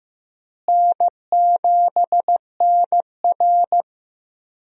Morse code, heard as a keyed tone, about 15 wpm, 700 Hz. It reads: N7NR
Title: USA Call Signs - 4 Characters - Repeated 2x - 15wpm